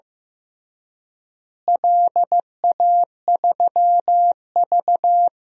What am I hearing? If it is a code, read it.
Morse code, heard as a keyed tone, about 15 wpm, 700 Hz. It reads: LA3V
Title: DXCC-100 Call Signs - 4 Characters - Repeated 3x - 15wpm